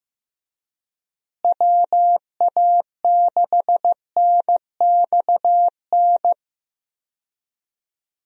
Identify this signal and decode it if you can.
Morse code, heard as a keyed tone, about 15 wpm, 700 Hz. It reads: WA6NXN
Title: USA Call Signs - 6 Characters - Repeated 2x - 15wpm